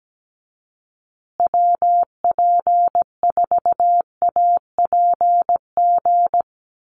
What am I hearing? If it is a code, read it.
Morse code, heard as a keyed tone, about 17 wpm, 700 Hz. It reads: WP4APG
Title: USA Call Signs Repeated 3x 17wpm